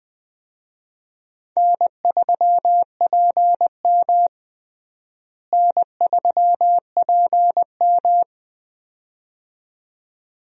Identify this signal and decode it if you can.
Morse code, heard as a keyed tone, about 20 wpm, 700 Hz. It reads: N3PM N3PM
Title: USA Call Signs - 4 Characters - Repeated 2x - 20wpm